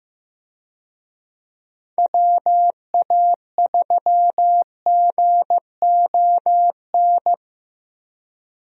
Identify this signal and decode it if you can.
Morse code, heard as a keyed tone, about 15 wpm, 700 Hz. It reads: WA3GON
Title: USA Call Signs - 6 Characters - Repeated 2x - 15wpm